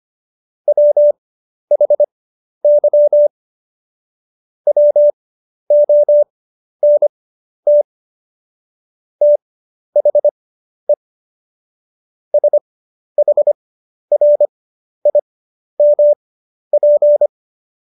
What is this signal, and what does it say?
Morse code, heard as a keyed tone, about 25 wpm, spaced as for 11 wpm, 590 Hz. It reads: WHY WONT THE SHRIMP
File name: Jokes-25-char-10-eff